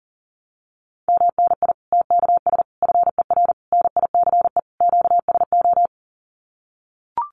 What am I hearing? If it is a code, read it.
Morse code, heard as a keyed tone, about 37 wpm, 700 Hz. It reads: MNI TKS FER NICE QSO
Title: Common QSO Phrases 40wpm